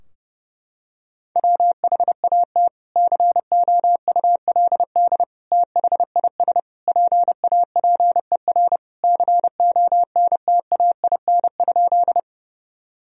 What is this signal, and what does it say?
Morse code, heard as a keyed tone, about 32 wpm, 700 Hz. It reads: WHAT COULD THIS PAPER CONTAIN?
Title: Sentences from Top 1000 Words - Review Part 2 - 30wpm